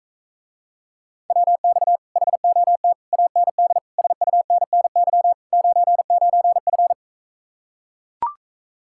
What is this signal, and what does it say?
Morse code, heard as a keyed tone, about 44 wpm, 700 Hz. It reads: WX HOT AND SUNNY 99F